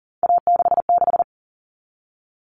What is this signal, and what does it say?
Morse code, heard as a keyed tone, about 40 wpm, 700 Hz. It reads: A66
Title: All Cut Numbers - 3-Digit 40wpm